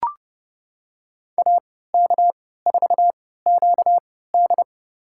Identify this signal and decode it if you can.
Morse code, heard as a keyed tone, about 30 wpm, spaced as for 17 wpm, 700 Hz. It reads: AK4QD
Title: USA Call Signs - 5 Characters - 3x Character Spacing - 30wpm